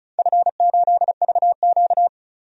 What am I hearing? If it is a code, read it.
Morse code, heard as a keyed tone, about 36 wpm, 700 Hz. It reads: F8VQ
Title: DXCC-100 Call Signs - 4 Characters - 35wpm